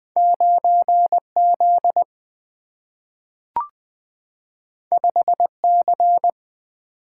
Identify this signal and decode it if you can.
Morse code, heard as a keyed tone, about 20 wpm, 700 Hz. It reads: 9Z 5C
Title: Single-Letter Single-Number 20wpm